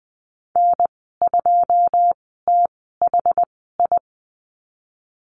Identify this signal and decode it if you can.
Morse code, heard as a keyed tone, about 19 wpm, spaced as for 14 wpm, 700 Hz. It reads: N2THI
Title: 088 Lesson 15g - Number 5 - Call Signs 20wpm